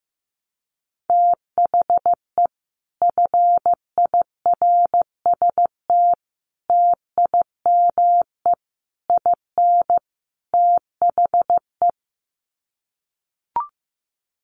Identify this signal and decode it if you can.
Morse code, heard as a keyed tone, about 15 wpm, 700 Hz. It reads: THE FIRST TIME IN THE
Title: Sets of 5 Words Encore 15wpm